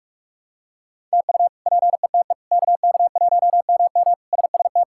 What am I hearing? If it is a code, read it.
Morse code, heard as a keyed tone, about 46 wpm, 700 Hz. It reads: TU PETE KK1MM SST
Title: K1USN SST QSO Practice - 45wpm